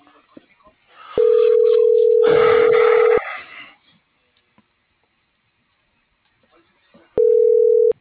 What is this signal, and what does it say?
Ringback tone (North America), 440 + 480 Hz.